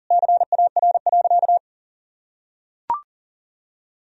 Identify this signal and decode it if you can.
Morse code, heard as a keyed tone, about 40 wpm, 700 Hz. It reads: CAR.
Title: Sentences from Top 1000 Words - Review Part 1 - 40wpm